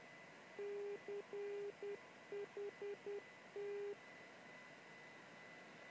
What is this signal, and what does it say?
Morse code, heard as a keyed tone, about 10 wpm, 410 Hz, with noise. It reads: CHT
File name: chiltern-non-directional-beacon-callsign-cht-in-morse-code-broadcasting-on-277-khz-recording-from-278-khz-on-lw